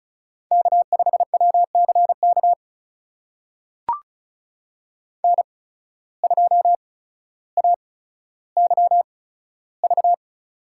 Morse code, heard as a keyed tone, about 34 wpm, spaced as for 15 wpm, 700 Hz. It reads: K5WCK N2AYV
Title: USA Call Signs - 8x Character Spacing - 35wpm